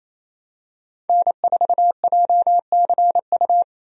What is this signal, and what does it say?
Morse code, heard as a keyed tone, about 28 wpm, 700 Hz. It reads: N4JCU
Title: USA Call Signs - 5 Characters - 28wpm